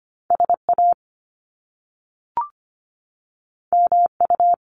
Morse code, heard as a keyed tone, about 25 wpm, 700 Hz. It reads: SA MU
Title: Top Two Letter Combinations in Words 25wpm